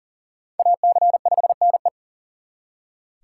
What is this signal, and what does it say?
Morse code, heard as a keyed tone, about 51 wpm, 700 Hz. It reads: AC5NE